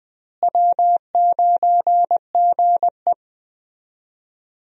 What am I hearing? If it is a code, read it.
Morse code, heard as a keyed tone, about 21 wpm, 700 Hz. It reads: W9GE
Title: USA Call Signs - 4 Characters - Repeated 2x - 20wpm